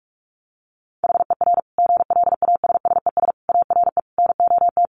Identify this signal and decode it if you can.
Morse code, heard as a keyed tone, about 44 wpm, 700 Hz. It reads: HER GLASSES ARE NOT
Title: Additional Easy Phrases 45wpm